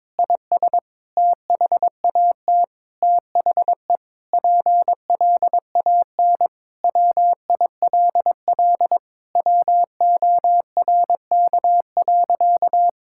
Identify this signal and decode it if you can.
Morse code, heard as a keyed tone, about 22 wpm, 700 Hz. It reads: IS THAT THE PLAN WILL WORK.
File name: Sentences with 7-Word Phrases - 22wpm